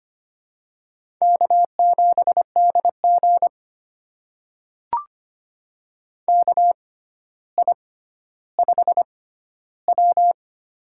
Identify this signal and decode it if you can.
Morse code, heard as a keyed tone, about 25 wpm, spaced as for 13 wpm, 700 Hz. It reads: K7DG KI5W
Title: USA Call Signs - 4 Characters - 6x Character Spacing - 25wpm